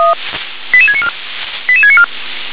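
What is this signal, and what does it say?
Five-tone selective call (ZVEI), heard as quick runs of five tones, tones of about 70 ms. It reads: 88694 88694
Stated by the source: ZVEI five-tone selective call received by SDR.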